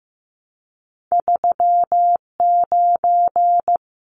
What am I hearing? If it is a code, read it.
Morse code, heard as a keyed tone, about 15 wpm, 700 Hz. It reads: 39
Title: Numbers - 2-Digit 15wpm